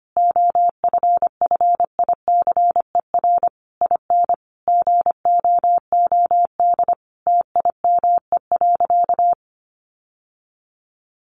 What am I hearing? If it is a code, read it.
Morse code, heard as a keyed tone, about 25 wpm, 700 Hz. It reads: OFFICER IN GOOD TIME.
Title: Sentences from Top 1200 Words 25wpm